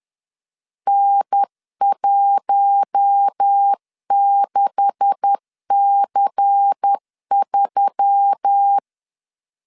Morse code, heard as a keyed tone, about 10 wpm, 800 Hz. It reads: N16C3
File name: LEZIONI / 9 286 10wpm 2735